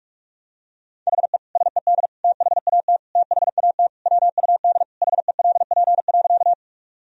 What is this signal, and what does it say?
Morse code, heard as a keyed tone, about 41 wpm, 700 Hz. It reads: HE SED THAT THAT WUD HELP.
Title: Sentences from Top 200 Words and Abbreviations 45wpm